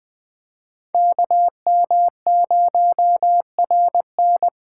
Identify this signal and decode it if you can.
Morse code, heard as a keyed tone, about 21 wpm, 700 Hz. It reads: KM0RN
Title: Speed Racing - USA 2x2 Call Signs 17wpm